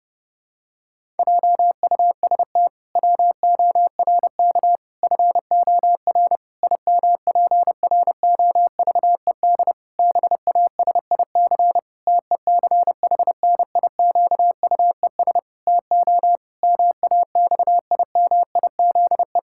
Morse code, heard as a keyed tone, about 30 wpm, 700 Hz. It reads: JUST WORK FOR IMPROVED BASIC TECHNIQUES TO MAXIMIZE
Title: Pangram Challenge v2 - 30wpm